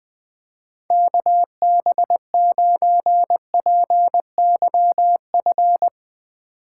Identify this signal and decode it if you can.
Morse code, heard as a keyed tone, about 20 wpm, 700 Hz. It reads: KB9PYF
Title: USA Call Signs - 6 Characters - 8x Character Spacing - 20wpm